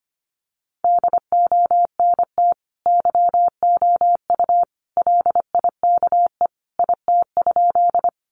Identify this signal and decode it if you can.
Morse code, heard as a keyed tone, about 25 wpm, 700 Hz. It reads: DONT YOU LIKE IT?
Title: Sentences from Top 300 Words 25wpm